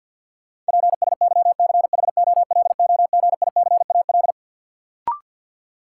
Morse code, heard as a keyed tone, about 52 wpm, 700 Hz. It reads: PSYCHOLOGICAL
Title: Sentences from Top 1400 Words - New Word Drill Alt - 50wpm